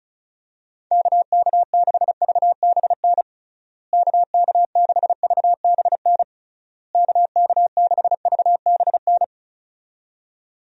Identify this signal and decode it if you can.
Morse code, heard as a keyed tone, about 36 wpm, 700 Hz. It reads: KK6VBN KK6VBN KK6VBN
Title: USA Call Signs - 6 Characters - Repeated 3x - 35wpm